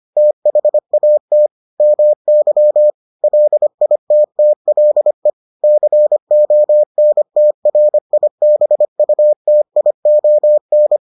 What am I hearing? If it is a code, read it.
Morse code, heard as a keyed tone, about 25 wpm, 600 Hz. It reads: THAT MY LITTLE CONTRIBUTION